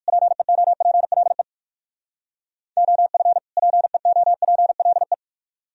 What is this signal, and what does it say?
Morse code, heard as a keyed tone, about 57 wpm, 700 Hz. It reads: PEOPLE OF PEOPLE